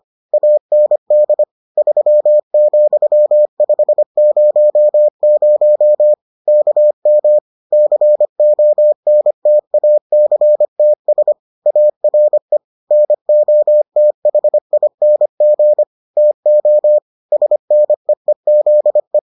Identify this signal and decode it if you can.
Morse code, heard as a keyed tone, about 25 wpm, 600 Hz. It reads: AND 3,500 KM CONTACTS ARE NOTHING TO SNEEZE